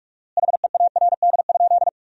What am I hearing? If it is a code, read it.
Morse code, heard as a keyed tone, about 47 wpm, 700 Hz. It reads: HEARD?
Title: 4x Word-Spacing - Sentences from Top 400 Words - 45wpm